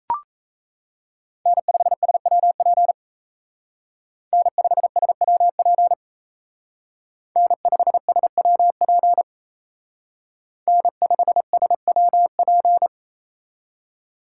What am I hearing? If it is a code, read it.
Morse code, heard as a keyed tone, about 35 wpm, 700 Hz. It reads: N5SWP N5SWP N5SWP N5SWP